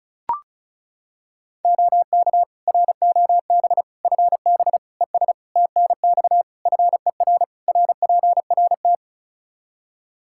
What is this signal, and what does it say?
Morse code, heard as a keyed tone, about 34 wpm, 700 Hz. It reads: OK ROB FB ES TNX FER RPRT
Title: Practice QSO - Exchange 2 - Info - 35wpm